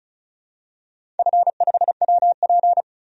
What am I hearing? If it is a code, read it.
Morse code, heard as a keyed tone, about 34 wpm, 700 Hz. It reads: F5WP
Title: DXCC-100 Call Signs - 4 Characters - 35wpm